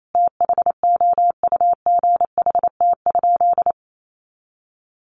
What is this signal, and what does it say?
Morse code, heard as a keyed tone, about 28 wpm, 700 Hz. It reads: THOUGHT?
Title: Sentences from Top 300 Words 28wpm